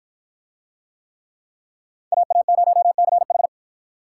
Morse code, heard as a keyed tone, about 53 wpm, 700 Hz. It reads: AA0CH